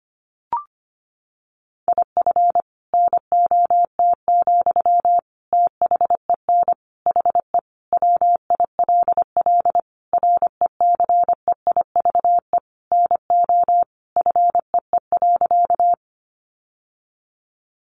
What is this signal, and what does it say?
Morse code, heard as a keyed tone, about 25 wpm, 700 Hz. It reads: IF NOT, THEN HE WILL RECEIVE NO FEE.
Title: Sentences from Top 1500 Words 25wpm